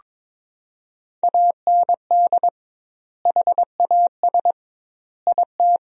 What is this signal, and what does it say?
Morse code, heard as a keyed tone, about 22 wpm, 700 Hz. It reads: AND HAS IT
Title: 2x Word-Spacing - Sentences from Top 100 Words - 22wpm